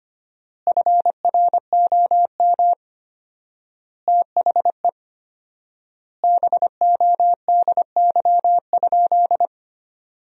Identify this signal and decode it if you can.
Morse code, heard as a keyed tone, about 25 wpm, 700 Hz. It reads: FROM THE BODY?
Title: 4x Word-Spacing - Sentences from Top 400 Words - 25wpm